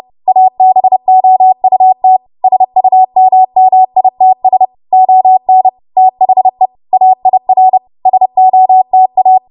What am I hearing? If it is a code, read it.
Morse code, heard as a keyed tone, about 30 wpm, 750 Hz. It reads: ABOUT SUMMITS ON THE AIR SOTA
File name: Morse / 30WPM